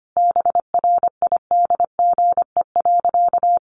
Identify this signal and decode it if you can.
Morse code, heard as a keyed tone, about 25 wpm, 700 Hz. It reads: BRIDGE.
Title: Sentences from Top 1500 Words 25wpm